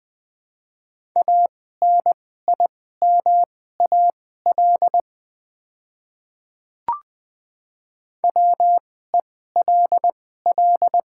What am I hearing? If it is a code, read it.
Morse code, heard as a keyed tone, about 20 wpm, spaced as for 15 wpm, 700 Hz. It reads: ANIMAL WELL
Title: 2x Character Spacing - Top 200 Words - 20wpm